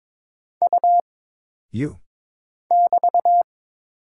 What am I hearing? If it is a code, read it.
Morse code, heard as a keyed tone, about 22 wpm, 700 Hz. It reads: U <BT>
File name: Single Character (Letter:Number:Punctuation:Prosign) - Mind-Melt - 22wpm